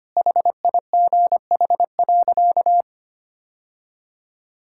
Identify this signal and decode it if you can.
Morse code, heard as a keyed tone, about 25 wpm, 700 Hz. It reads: HIGH.
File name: practice / 25wpm / Sentences with 3-Word Phrases - 25wpm